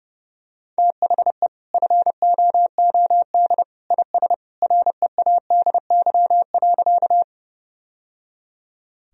Morse code, heard as a keyed tone, about 30 wpm, 700 Hz. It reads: THE FOOD IS READY.
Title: Sentences from Top 1000 Words - Review Part 1 - 30wpm